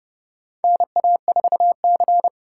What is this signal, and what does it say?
Morse code, heard as a keyed tone, about 30 wpm, 700 Hz. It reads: NA4C
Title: K1USN SST QSO Practice - 30wpm